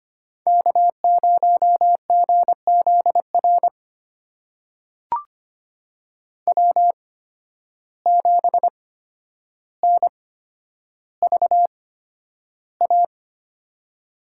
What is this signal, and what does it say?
Morse code, heard as a keyed tone, about 25 wpm, spaced as for 11 wpm, 700 Hz. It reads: K0GZR W7NVA
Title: USA Call Signs - 5 Characters - 8x Character Spacing - 25wpm